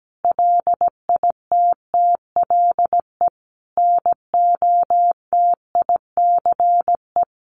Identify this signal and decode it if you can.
Morse code, heard as a keyed tone, about 17 wpm, 700 Hz. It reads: LITTLE NOTICE
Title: Sentences from Top 400 Words 17wpm